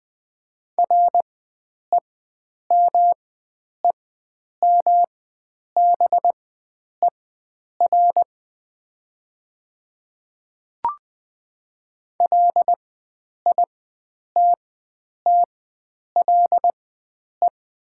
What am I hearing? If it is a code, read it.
Morse code, heard as a keyed tone, about 20 wpm, spaced as for 9 wpm, 700 Hz. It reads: REMEMBER LITTLE
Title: 4x Character Spacing - Top 500 Words - 20wpm